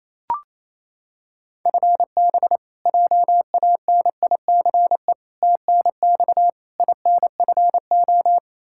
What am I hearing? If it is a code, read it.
Morse code, heard as a keyed tone, about 27 wpm, 700 Hz. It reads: FB JANICE TNX INFO
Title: K1USN SST QSO Practice 28wpm